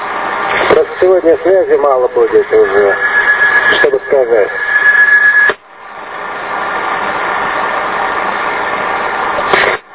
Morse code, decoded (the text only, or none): none